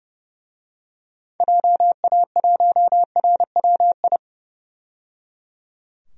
Morse code, heard as a keyed tone, about 30 wpm, 700 Hz. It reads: JA1RWI
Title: DXCC-100 Call Signs - 6 Characters - 30wpm